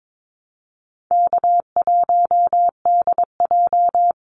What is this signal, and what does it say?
Morse code, heard as a keyed tone, about 22 wpm, 700 Hz. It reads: K1DJ
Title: Easy Call Signs Repeated 2x - 22wpm